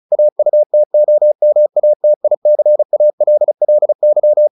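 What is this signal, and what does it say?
Morse code, heard as a keyed tone, about 34 wpm, 600 Hz. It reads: AUTOMATICALLY
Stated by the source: Morse practice at 35 wpm from CW med Gnister, sending Ken Follett's novel Never in English.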